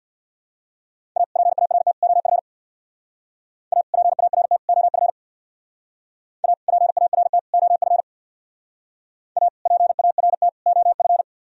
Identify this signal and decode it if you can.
Morse code, heard as a keyed tone, about 66 wpm, 700 Hz. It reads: A PART OF A PART OF A PART OF A PART OF